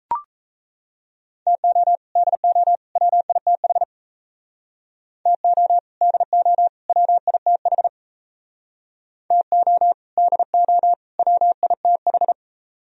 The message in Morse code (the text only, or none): TO DO WITH TO DO WITH TO DO WITH